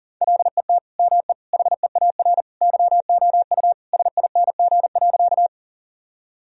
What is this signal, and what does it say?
Morse code, heard as a keyed tone, about 40 wpm, 700 Hz. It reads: LET ME HEAR YOU SING.
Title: Sentences from Top 1000 Words - Review Part 2 - 40wpm